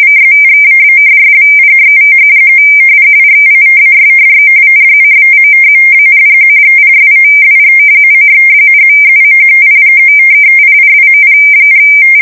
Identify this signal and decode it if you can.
Two-tone teleprinter signal, about 170 Hz shift.
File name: rtty / random170